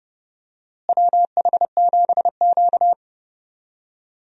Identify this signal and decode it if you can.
Morse code, heard as a keyed tone, about 30 wpm, 700 Hz. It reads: WH7Q
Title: USA Call Signs - 4 Characters - Repeated 3x - 30wpm